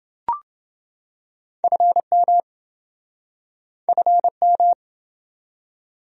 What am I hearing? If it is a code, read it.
Morse code, heard as a keyed tone, about 27 wpm, 700 Hz. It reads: FM FM